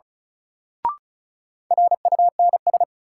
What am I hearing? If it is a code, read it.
Morse code, heard as a keyed tone, about 34 wpm, 700 Hz. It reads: RUNS
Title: Sentences from Top 600 Words - New Word Drill - 35wpm